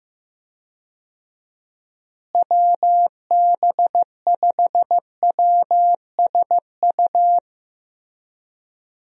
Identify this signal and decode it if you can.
Morse code, heard as a keyed tone, about 15 wpm, 700 Hz. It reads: WB5WSU
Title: USA Call Signs - 6 Characters - Repeated 2x - 15wpm